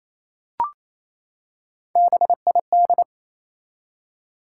Morse code, heard as a keyed tone, about 28 wpm, 700 Hz. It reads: BID
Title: Three Letter Words - 28wpm